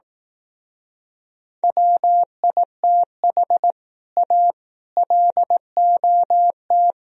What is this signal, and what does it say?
Morse code, heard as a keyed tone, about 18 wpm, 700 Hz. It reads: WITH A LOT
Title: Speed-Racing - Sets of 4 Words 15wpm